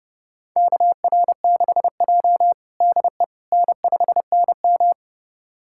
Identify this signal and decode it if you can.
Morse code, heard as a keyed tone, about 30 wpm, 700 Hz. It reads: KR6J DE N5NM
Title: USA Callsign DE Callsign - 4 Characters - 30wpm